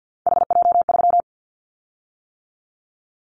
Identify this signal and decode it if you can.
Morse code, heard as a keyed tone, about 50 wpm, 700 Hz. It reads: HJ3
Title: Callsign Prefixes for Top 100 DXCC - 50wpm